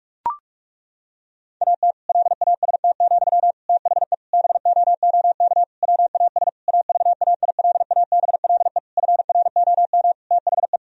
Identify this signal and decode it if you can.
Morse code, heard as a keyed tone, about 41 wpm, 700 Hz. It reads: AT LAST, THE BOOK WAS AVAILABLE FROM THE
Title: Sentences from Top 1000 Words - Review Part 3 - 45wpm